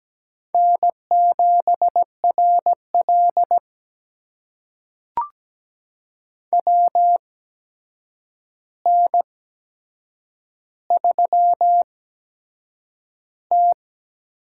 Morse code, heard as a keyed tone, about 17 wpm, spaced as for 7 wpm, 700 Hz. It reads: N7RL WN3T
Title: USA Call Signs - 4 Characters - 8x Character Spacing - 17wpm